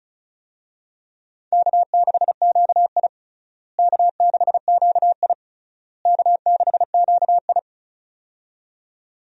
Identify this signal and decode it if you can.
Morse code, heard as a keyed tone, about 36 wpm, 700 Hz. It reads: K6QI K6QI K6QI
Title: USA Call Signs - 4 Characters - Repeated 3x - 35wpm